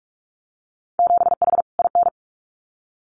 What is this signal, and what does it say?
Morse code, heard as a keyed tone, about 46 wpm, 700 Hz. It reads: 7H IN
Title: Winter Field Day Exchanges Encore 45wpm